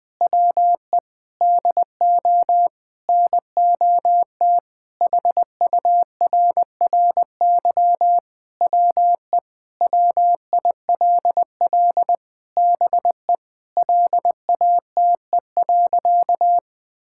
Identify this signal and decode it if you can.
Morse code, heard as a keyed tone, about 21 wpm, 700 Hz. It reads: WE DO NOT HURRY WE WILL BE LATE.
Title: Sentences with 4-Word Phrases - 20wpm